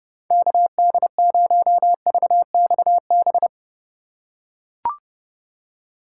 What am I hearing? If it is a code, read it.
Morse code, heard as a keyed tone, about 29 wpm, 700 Hz. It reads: KD0VXB
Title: USA Call Signs - 6 Characters - 2x Character Spacing - 30wpm